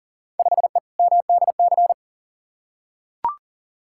Morse code, heard as a keyed tone, about 40 wpm, 700 Hz. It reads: 5E MDC
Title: Field Day Exchanges - 40wpm